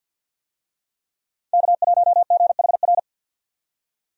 Morse code, heard as a keyed tone, about 53 wpm, 700 Hz. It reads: K1GHR
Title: USA Call Signs - 5 Characters - 50wpm